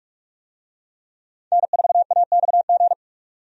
Morse code, heard as a keyed tone, about 46 wpm, 700 Hz. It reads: N4AXG
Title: USA Call Signs - 5 Characters - 45wpm